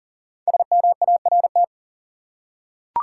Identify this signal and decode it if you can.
Morse code, heard as a keyed tone, about 40 wpm, 700 Hz. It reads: SMART
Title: Five Letter Words - 40wpm